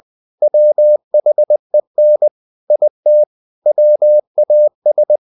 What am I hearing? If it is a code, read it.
Morse code, heard as a keyed tone, about 20 wpm, 600 Hz. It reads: WHEN IT WAS
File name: the old man and the sea / the old man and the sea part 1 20wpm